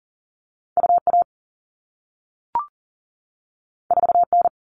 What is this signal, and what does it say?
Morse code, heard as a keyed tone, about 42 wpm, 700 Hz. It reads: UA 4N
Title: All Cut Numbers - 2-Digit 40wpm